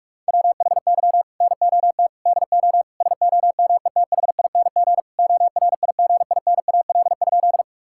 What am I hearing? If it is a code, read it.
Morse code, heard as a keyed tone, about 41 wpm, 700 Hz. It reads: WHY NOT DO SOMETHING ORIGINAL?
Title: Sentences from Top 1000 Words - Review Part 6 - 45wpm